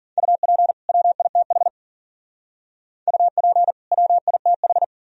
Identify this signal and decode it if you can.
Morse code, heard as a keyed tone, about 45 wpm, 700 Hz. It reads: UP WITH UP WITH
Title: Speed-Racing - Sets of 2 Words 40wpm